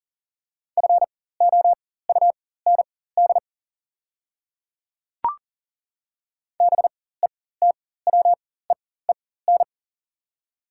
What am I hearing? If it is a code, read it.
Morse code, heard as a keyed tone, about 40 wpm, spaced as for 19 wpm, 690 Hz. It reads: FOUND BETWEEN
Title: 4x Character Spacing - Top 300 Words - 40wpm